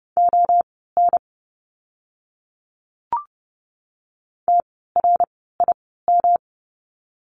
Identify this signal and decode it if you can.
Morse code, heard as a keyed tone, about 30 wpm, spaced as for 17 wpm, 700 Hz. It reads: ON TRIM
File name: course / 30:10wpm / 108 Lesson 18f - Letter m - Words 30wpm